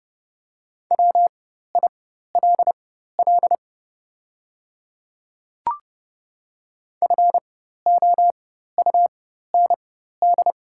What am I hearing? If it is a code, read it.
Morse code, heard as a keyed tone, about 30 wpm, spaced as for 14 wpm, 700 Hz. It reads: WILL FOUND
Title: 4x Character Spacing - Top 500 Words - 30wpm